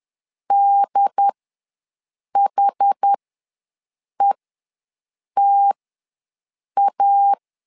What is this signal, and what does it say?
Morse code, heard as a keyed tone, about 10 wpm, spaced as for 6 wpm, 800 Hz. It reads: DHETA
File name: LEZIONI / 3 MWRO / 3 M ws 10wpm 2121